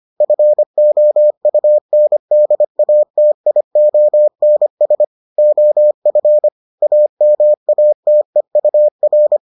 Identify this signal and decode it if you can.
Morse code, heard as a keyed tone, about 25 wpm, 600 Hz. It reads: FOUNDATIONS OF AMATEUR